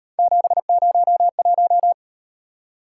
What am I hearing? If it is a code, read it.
Morse code, heard as a keyed tone, about 35 wpm, 700 Hz. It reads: 701